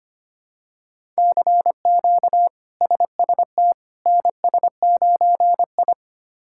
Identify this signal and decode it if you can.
Morse code, heard as a keyed tone, about 25 wpm, 700 Hz. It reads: CQ SST NS9I
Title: K1USN SST QSO Practice - 25wpm